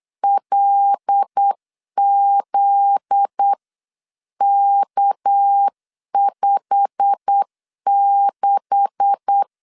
Morse code, heard as a keyed tone, about 8 wpm, 800 Hz. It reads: LZK56
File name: LEZIONI / 9 286 8wpm 3425